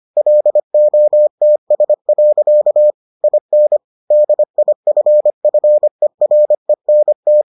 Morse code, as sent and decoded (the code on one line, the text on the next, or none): .-.. --- - ... .-.-.- / .. -. / -.. .. ..-. ..-. .-. -. -
LOTS. IN DIFFERENT